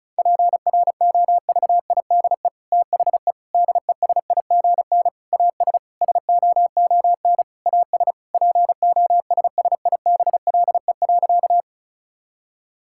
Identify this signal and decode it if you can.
Morse code, heard as a keyed tone, about 34 wpm, 700 Hz. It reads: PROVIDE THE DESIGN AS SOON AS POSSIBLE.
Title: Sentences from Top 1000 Words - Review Part 5 - 35wpm